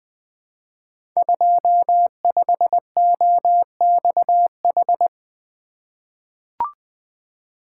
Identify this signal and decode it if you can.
Morse code, heard as a keyed tone, about 20 wpm, 700 Hz. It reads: 25OXH
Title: Code Groups - 5 Letter:Numbers - 20wpm